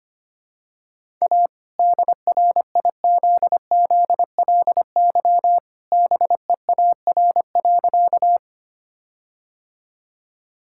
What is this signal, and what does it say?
Morse code, heard as a keyed tone, about 25 wpm, 700 Hz. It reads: A DRIZZLY BEAR.
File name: practice / 25wpm / All-Age Appropriate Jokes - 25wpm